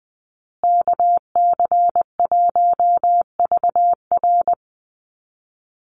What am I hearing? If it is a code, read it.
Morse code, heard as a keyed tone, about 20 wpm, 700 Hz. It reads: KC1VR
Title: USA Call Signs Repeated 3x 20wpm